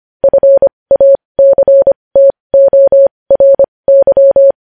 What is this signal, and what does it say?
Morse code, heard as a keyed tone, about 25 wpm, spaced as for 20 wpm, 550 Hz. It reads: FACTORY
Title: News Headlines at 20